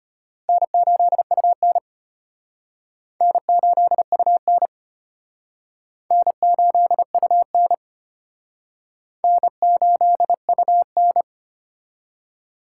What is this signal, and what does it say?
Morse code, heard as a keyed tone, about 32 wpm, 700 Hz. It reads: N8UN N8UN N8UN N8UN